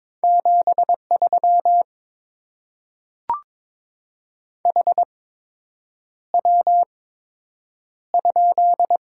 Morse code, heard as a keyed tone, about 22 wpm, spaced as for 8 wpm, 700 Hz. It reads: 73 HW?
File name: Top 20 QSO Elements - 8x Character Spacing - 22wpm